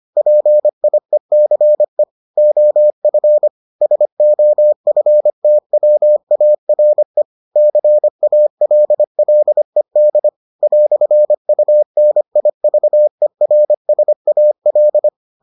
Morse code, heard as a keyed tone, about 25 wpm, 600 Hz. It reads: PIECE OF SOFTWARE CALLED "UNIVERSAL